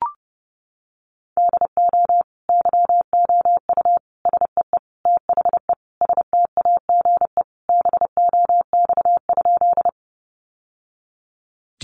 Morse code, heard as a keyed tone, about 30 wpm, 700 Hz. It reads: DO YOU SEE THE STAGE BOX?
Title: Sentences from Top 1200 Words 30wpm